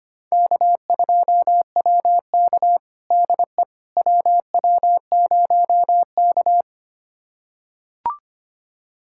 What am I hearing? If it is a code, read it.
Morse code, heard as a keyed tone, about 25 wpm, 700 Hz. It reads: K2WK DE WW0K
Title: USA Callsign DE Callsign - 4 Characters - 25wpm